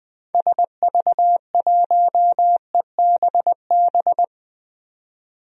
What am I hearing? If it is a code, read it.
Morse code, heard as a keyed tone, about 20 wpm, 700 Hz. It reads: SV1EBB
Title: DXCC-100 Call Signs - 6 Characters - 20wpm